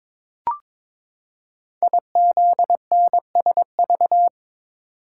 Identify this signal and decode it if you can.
Morse code, heard as a keyed tone, about 22 wpm, 700 Hz. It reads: IZNSV
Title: Code Groups - 5 Letters - 22wpm